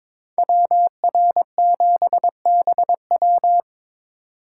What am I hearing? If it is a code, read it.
Morse code, heard as a keyed tone, about 26 wpm, 700 Hz. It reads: WR7BW
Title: Speed Racing - USA 2x2 Call Signs 22wpm